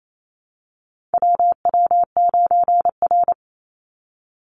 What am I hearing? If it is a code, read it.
Morse code, heard as a keyed tone, about 28 wpm, 700 Hz. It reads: WW9R
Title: Call Signs Repeated 3x - 28wpm